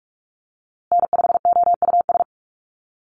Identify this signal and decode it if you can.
Morse code, heard as a keyed tone, about 44 wpm, 700 Hz. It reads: N5OUS
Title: USA Call Signs Repeated 2x 45wpm